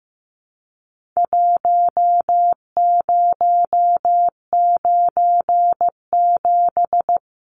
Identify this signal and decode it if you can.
Morse code, heard as a keyed tone, about 15 wpm, 700 Hz. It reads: 1097
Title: Numbers - Contest Style - 4-Digit 15wpm